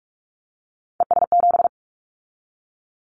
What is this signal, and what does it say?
Morse code, heard as a keyed tone, about 44 wpm, 700 Hz. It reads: ES7
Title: Callsign Prefixes for Top 50 DXCC - 45wpm